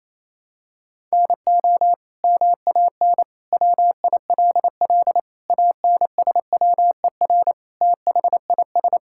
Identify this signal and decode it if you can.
Morse code, heard as a keyed tone, about 28 wpm, 700 Hz. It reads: NO MAN WILL ANSWER THIS
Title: Sentences from Top 1000 Words - Review Part 6 - 28wpm